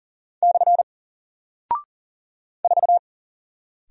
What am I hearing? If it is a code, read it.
Morse code, heard as a keyed tone, about 40 wpm, 700 Hz. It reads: / 4
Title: Single Character (Letter:Number:Punctuation:Prosign) 40wpm